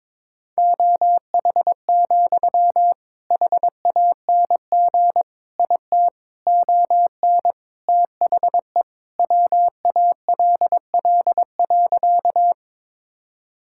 Morse code, heard as a keyed tone, about 22 wpm, 700 Hz. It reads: OH, HANG IT ON THE WALL.
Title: Sentences from Top 1600 Words 22wpm